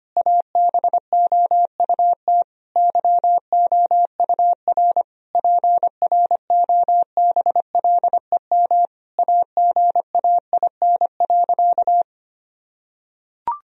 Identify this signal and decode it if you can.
Morse code, heard as a keyed tone, about 25 wpm, 700 Hz. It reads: ABOUT YOUR PROBLEM AGAIN.
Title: Sentences from Top 1000 Words - Review Part 3 - 25wpm